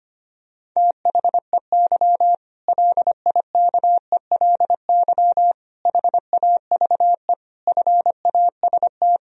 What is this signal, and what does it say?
Morse code, heard as a keyed tone, about 25 wpm, 700 Hz. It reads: THEY LIKELY HAVE FAST
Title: Sentences from Top 1000 Words - Review Part 4 - 25wpm